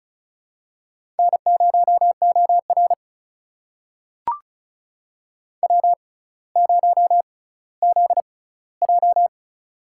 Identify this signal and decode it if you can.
Morse code, heard as a keyed tone, about 36 wpm, spaced as for 18 wpm, 700 Hz. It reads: N0OR W0ZJ
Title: USA Call Signs - 4 Characters - 6x Character Spacing - 35wpm